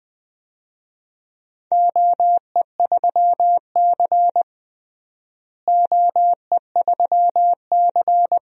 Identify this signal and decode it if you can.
Morse code, heard as a keyed tone, about 20 wpm, 700 Hz. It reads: OE3C OE3C
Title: DXCC-100 Call Signs - 4 Characters - Repeated 3x - 20wpm